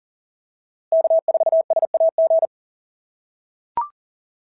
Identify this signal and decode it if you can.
Morse code, heard as a keyed tone, about 37 wpm, 680 Hz. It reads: K4SAG
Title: SOTA QSO Practice 40wpm